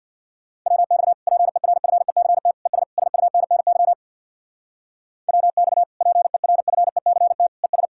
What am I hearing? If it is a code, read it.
Morse code, heard as a keyed tone, about 60 wpm, 700 Hz. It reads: WX PERFECT ES SUNNY WX PERFECT ES
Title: Speed-Racing Common QSO Phrases - 50wpm